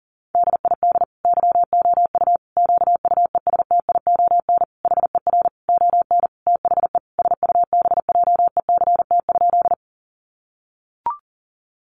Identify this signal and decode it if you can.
Morse code, heard as a keyed tone, about 40 wpm, 700 Hz. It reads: DID YOU QUESTION HER ON THE SUBJECT?
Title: Sentences from Top 800 Words 40wpm